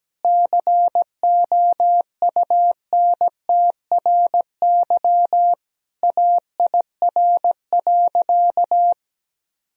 Morse code, heard as a keyed tone, about 17 wpm, 700 Hz. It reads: COUNTRY AIR.
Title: Sentences from Top 1000 Words - Review Part 1 - 17wpm